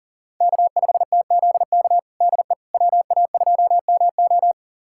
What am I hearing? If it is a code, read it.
Morse code, heard as a keyed tone, about 37 wpm, 700 Hz. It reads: K5TZK DE WA2MO
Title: USA Callsign DE Callsign - 5 Characters - 40wpm